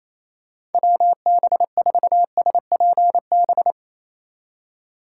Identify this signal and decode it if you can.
Morse code, heard as a keyed tone, about 28 wpm, 700 Hz. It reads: WB4SPB
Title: K1USN SST QSO Practice - 28wpm